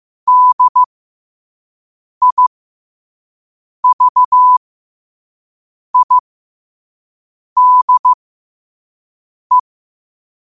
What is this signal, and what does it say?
Morse code, heard as a keyed tone, about 15 wpm, spaced as for 5 wpm, 1000 Hz. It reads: DIVIDE
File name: qotdfiles / 5wpm